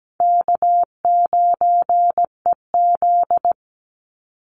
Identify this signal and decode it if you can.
Morse code, heard as a keyed tone, about 17 wpm, 700 Hz. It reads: K9EZ
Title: NA SOTA Chaser Callsigns 17wpm